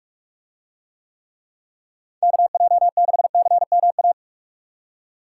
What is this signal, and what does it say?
Morse code, heard as a keyed tone, about 45 wpm, 700 Hz. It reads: KJ6CMA